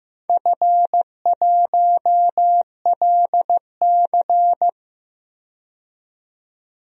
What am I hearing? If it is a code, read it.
Morse code, heard as a keyed tone, about 15 wpm, 700 Hz. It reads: F1LC
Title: DXCC-100 Call Signs - 4 Characters - 15wpm